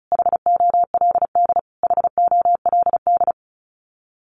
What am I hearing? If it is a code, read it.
Morse code, heard as a keyed tone, about 34 wpm, 700 Hz. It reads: HOLD HOLD
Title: Top 500 Words Repeated 35wpm